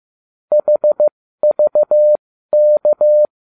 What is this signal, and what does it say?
Morse code, heard as a keyed tone, about 15 wpm, spaced as for 13 wpm, 600 Hz. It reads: HVK